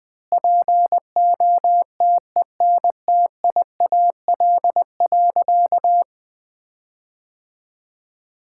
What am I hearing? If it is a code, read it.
Morse code, heard as a keyed tone, about 21 wpm, 700 Hz. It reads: POTENTIAL.
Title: Sentences with 9-Word Phrases - Spoken Beginning - 20wpm